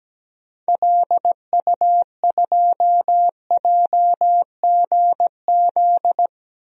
Morse code, heard as a keyed tone, about 17 wpm, 700 Hz. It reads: LU2JGZ
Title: DXCC-100 Call Signs - 6 Characters - 17wpm